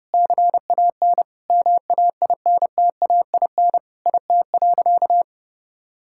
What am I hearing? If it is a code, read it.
Morse code, heard as a keyed tone, about 32 wpm, 700 Hz. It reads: CAN MAINTAIN IT.
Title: Sentences from Top 1000 Words - Review Part 5 - 30wpm